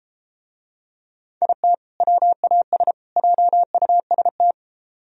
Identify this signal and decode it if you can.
Morse code, heard as a keyed tone, about 33 wpm, 700 Hz. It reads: IT WAS JUST